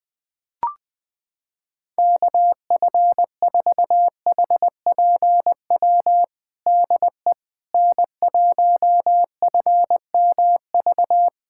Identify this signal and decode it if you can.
Morse code, heard as a keyed tone, about 21 wpm, 700 Hz. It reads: KF4HPW DE N1FMV
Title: USA Callsign DE Callsign 20wpm